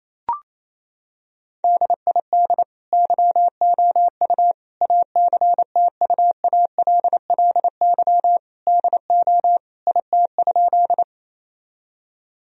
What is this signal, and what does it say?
Morse code, heard as a keyed tone, about 28 wpm, 700 Hz. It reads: DID YOU ACTUALLY DO IT?
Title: Sentences from Top 1000 Words - Review Part 4 - 28wpm